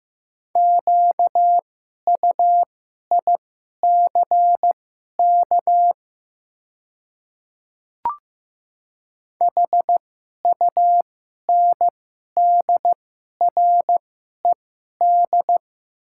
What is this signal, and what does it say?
Morse code, heard as a keyed tone, about 15 wpm, spaced as for 11 wpm, 700 Hz. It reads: QUICK HUNDRED
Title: 2x Character Spacing - Top 500 Words - 15wpm